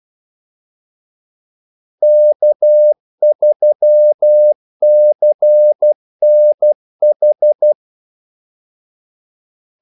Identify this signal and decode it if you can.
Morse code, heard as a keyed tone, about 12 wpm, 600 Hz. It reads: K3CNH
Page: 2 ekstra ordmellemrum mellem kaldesignalerne